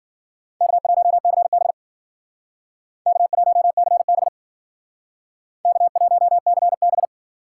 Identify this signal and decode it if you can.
Morse code, heard as a keyed tone, about 51 wpm, 700 Hz. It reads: K1CB K1CB K1CB